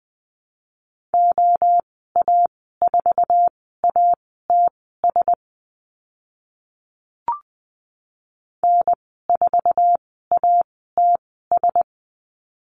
Morse code, heard as a keyed tone, about 20 wpm, spaced as for 14 wpm, 700 Hz. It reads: OA4ATS N4ATS
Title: 069 Lesson 12g - Letter d - Call Signs 20wpm